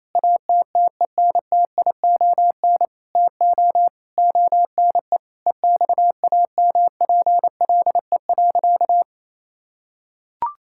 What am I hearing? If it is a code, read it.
Morse code, heard as a keyed tone, about 28 wpm, 700 Hz. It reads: ATTENTION TO ONE EXAMPLE.
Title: Sentences from Top 1000 Words - Review Part 6 - 28wpm